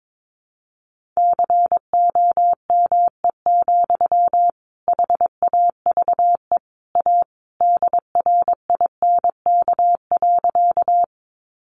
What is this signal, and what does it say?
Morse code, heard as a keyed tone, about 22 wpm, 700 Hz. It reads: COME, HAVE A DRINK.
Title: Sentences from Top 1500 Words 22wpm